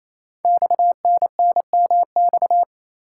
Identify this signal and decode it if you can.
Morse code, heard as a keyed tone, about 27 wpm, 700 Hz. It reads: XNNMX
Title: Code Groups - 5 Letters - 28wpm